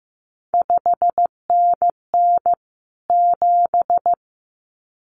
Morse code, heard as a keyed tone, about 15 wpm, 700 Hz. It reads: 5NN 7
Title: CQ WW DX Contest Exchange - 15wpm